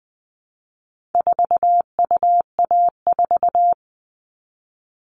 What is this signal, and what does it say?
Morse code, heard as a keyed tone, about 20 wpm, 700 Hz. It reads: 4UA4
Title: All Cut Numbers - 4-Digit 20wpm